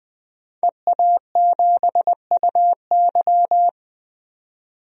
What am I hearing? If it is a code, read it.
Morse code, heard as a keyed tone, about 20 wpm, 700 Hz. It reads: EA7UY
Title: DXCC-100 Call Signs - 5 Characters - 20wpm